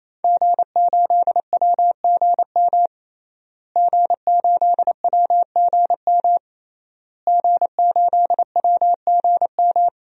Morse code, heard as a keyed tone, about 28 wpm, 700 Hz. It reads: G8WGM G8WGM G8WGM
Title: DXCC-100 Call Signs - 5 Characters - Repeated 3x - 28wpm